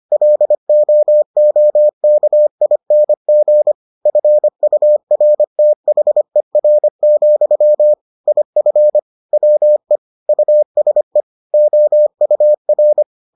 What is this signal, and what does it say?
Morse code, heard as a keyed tone, about 25 wpm, 600 Hz. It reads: LOOKING FURTHER, IF WE USE OUR